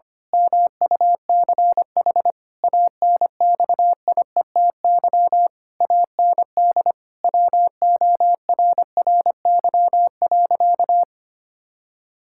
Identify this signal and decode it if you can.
Morse code, heard as a keyed tone, about 25 wpm, 700 Hz. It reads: MUCH ANXIETY AND WORRY.
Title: Pangram Challenge v2 - 25wpm